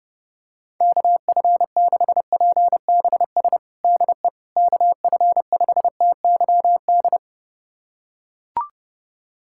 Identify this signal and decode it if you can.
Morse code, heard as a keyed tone, about 30 wpm, 700 Hz. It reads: KF6PBS DE KF5TYD
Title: USA Callsign DE Callsign - 6 Characters - 30wpm